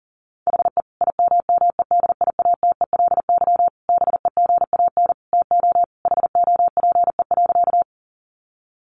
Morse code, heard as a keyed tone, about 40 wpm, 700 Hz. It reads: HE IMMEDIATELY BEGAN TO HOPE.
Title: Sentences from Top 1100 Words 40wpm